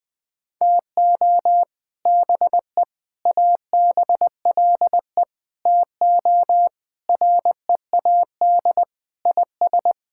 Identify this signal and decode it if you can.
Morse code, heard as a keyed tone, about 21 wpm, 700 Hz. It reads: TO BE ABLE TO READ IS
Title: Sentences with 4-Word Phrases - 20wpm